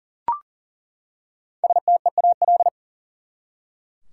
Morse code, heard as a keyed tone, about 40 wpm, 700 Hz. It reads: STEAL
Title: Five Letter Words - 40wpm